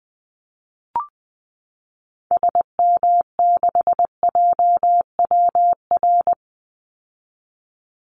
Morse code, heard as a keyed tone, about 20 wpm, 700 Hz. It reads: SM6JWR
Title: Call Signs Repeated 3x - 20wpm